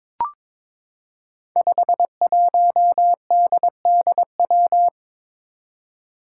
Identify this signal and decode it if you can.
Morse code, heard as a keyed tone, about 22 wpm, 700 Hz. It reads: 51DDW
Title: Code Groups - 5 Letter:Numbers - 22wpm